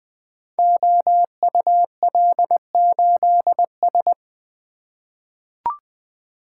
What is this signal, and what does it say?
Morse code, heard as a keyed tone, about 21 wpm, 700 Hz. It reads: OUL8S
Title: Code Groups - 5 Letter:Numbers - 20wpm